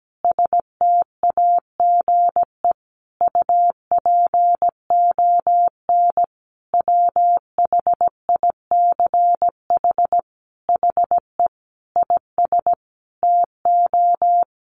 Morse code, heard as a keyed tone, about 17 wpm, 700 Hz. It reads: STAGE UPON WHICH HE IS TO
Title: Sentences from Top 1200 Words 17wpm